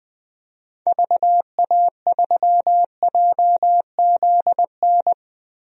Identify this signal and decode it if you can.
Morse code, heard as a keyed tone, about 21 wpm, 700 Hz. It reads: VA3JZN
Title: DXCC-100 Call Signs - 6 Characters - 20wpm